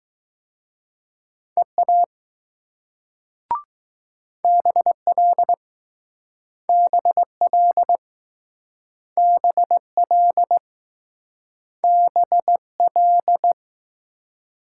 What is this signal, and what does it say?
Morse code, heard as a keyed tone, about 20 wpm, 700 Hz. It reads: EA BL BL BL BL